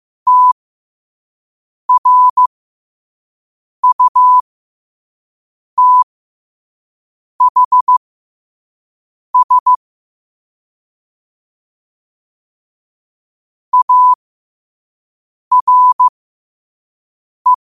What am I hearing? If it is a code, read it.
Morse code, heard as a keyed tone, about 15 wpm, spaced as for 5 wpm, 1000 Hz. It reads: TRUTHS ARE
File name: qotdfiles / 5wpm